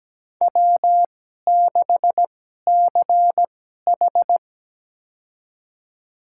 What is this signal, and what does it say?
Morse code, heard as a keyed tone, about 17 wpm, spaced as for 12 wpm, 700 Hz. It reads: W6CH
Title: USA Call Signs - 4 Characters - 2x Character Spacing - 17wpm